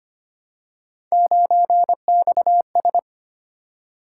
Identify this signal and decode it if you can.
Morse code, heard as a keyed tone, about 25 wpm, 700 Hz. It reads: 9XS
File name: Three Letter-Number Combinations - 25wpm